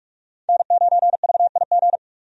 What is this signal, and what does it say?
Morse code, heard as a keyed tone, about 46 wpm, 700 Hz. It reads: N9VIG
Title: USA Call Signs - 5 Characters - 45wpm